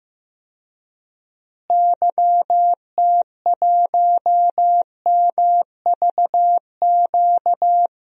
Morse code, heard as a keyed tone, about 15 wpm, 700 Hz. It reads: YT1MVQ
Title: DXCC-100 Call Signs - 6 Characters - Repeated 3x - 15wpm